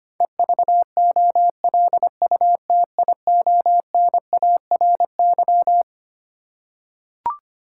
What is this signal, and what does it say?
Morse code, heard as a keyed tone, about 25 wpm, 700 Hz. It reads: EVOLUTIONARY
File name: Twelve Letter Words - 25wpm